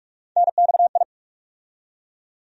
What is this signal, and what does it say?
Morse code, heard as a keyed tone, about 46 wpm, 700 Hz. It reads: NXI
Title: USA Call Sign Postfixes - 3 Letters 45wpm